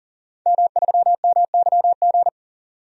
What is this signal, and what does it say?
Morse code, heard as a keyed tone, about 39 wpm, 700 Hz. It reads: M3MYG
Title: DXCC-100 Call Signs - 5 Characters - 40wpm